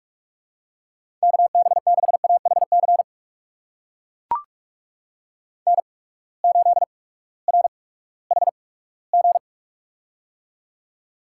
Morse code, heard as a keyed tone, about 44 wpm, spaced as for 20 wpm, 700 Hz. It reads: KB6AHC N8RHG
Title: USA Call Signs - 8x Character Spacing - 45wpm